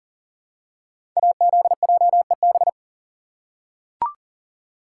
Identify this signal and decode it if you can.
Morse code, heard as a keyed tone, about 41 wpm, 700 Hz. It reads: AZJEB
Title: Code Groups - 5 Letter:Numbers - 40wpm